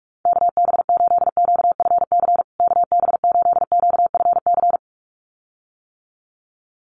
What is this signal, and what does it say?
Morse code, heard as a keyed tone, about 44 wpm, 700 Hz. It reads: KB8QFC KB8QFC
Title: USA Call Signs Repeated 2x 45wpm